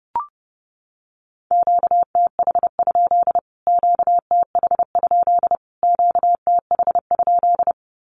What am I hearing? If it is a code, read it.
Morse code, heard as a keyed tone, about 30 wpm, 700 Hz. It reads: QTH? QTH? QTH?
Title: Q-Codes Repeated 3x 30wpm